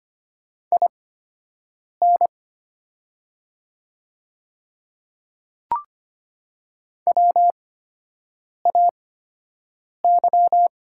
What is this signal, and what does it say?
Morse code, heard as a keyed tone, about 25 wpm, spaced as for 7 wpm, 700 Hz. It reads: IN WAY
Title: 8x Character Spacing - Top 100 Words - 25wpm